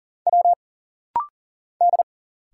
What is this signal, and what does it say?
Morse code, heard as a keyed tone, about 40 wpm, 700 Hz. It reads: W D
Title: Single Letters - Rapid-Fire 40wpm